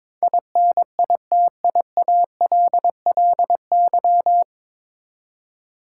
Morse code, heard as a keyed tone, about 22 wpm, 700 Hz. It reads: INITIALLY
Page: Morse Code Ninja practice 22wpm